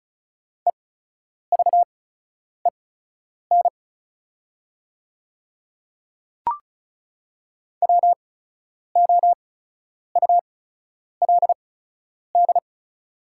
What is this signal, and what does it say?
Morse code, heard as a keyed tone, about 36 wpm, spaced as for 10 wpm, 700 Hz. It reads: EVEN WOULD
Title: 8x Character Spacing - Top 200 Words - 35wpm